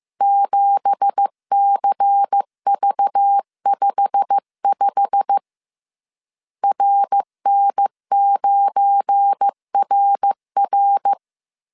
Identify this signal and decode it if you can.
Morse code, heard as a keyed tone, about 15 wpm, 800 Hz. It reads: 7CV55 RN9RR
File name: CW / DET / 1 9 1500C S C 14wpm 3245